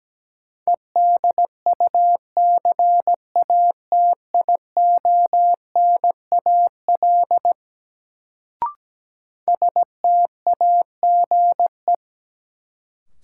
Morse code, heard as a keyed tone, about 17 wpm, 700 Hz. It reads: EDUCATIONAL STAGE
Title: Sentences from Top 1200 Words - New Word Drill - 17wpm